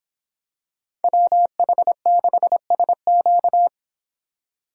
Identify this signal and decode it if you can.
Morse code, heard as a keyed tone, about 24 wpm, 700 Hz. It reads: WH6SQ